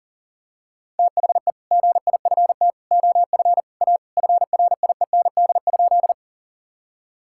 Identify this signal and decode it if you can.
Morse code, heard as a keyed tone, about 40 wpm, 700 Hz. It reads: THE GIFT OF A FRIEND?
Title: Sentences from Top 1000 Words - Review Part 6 - 40wpm